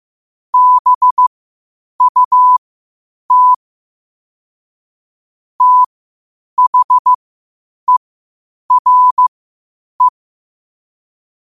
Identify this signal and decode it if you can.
Morse code, heard as a keyed tone, about 15 wpm, spaced as for 8 wpm, 1000 Hz. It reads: BUT THERE
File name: qotdfiles / 8wpm